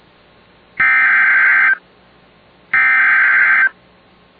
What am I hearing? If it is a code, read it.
Emergency Alert System (SAME) tones (header bursts), heard as harsh buzzing bursts. It is heard from 0.8 to 3.6 s.